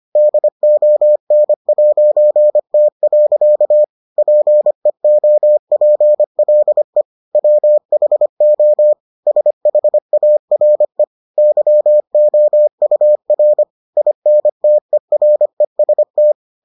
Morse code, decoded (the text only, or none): DON'T. PEOPLE WHO SHARE YOUR INTEREST